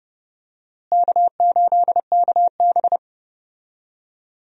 Morse code, heard as a keyed tone, about 30 wpm, 700 Hz. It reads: K8KB
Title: USA Call Signs - 4 Characters - Repeated 2x - 30wpm